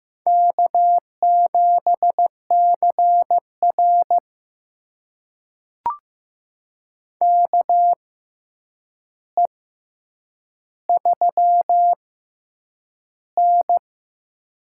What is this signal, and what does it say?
Morse code, heard as a keyed tone, about 15 wpm, spaced as for 8 wpm, 700 Hz. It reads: K7CR KE3N
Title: USA Call Signs - 4 Characters - 6x Character Spacing - 15wpm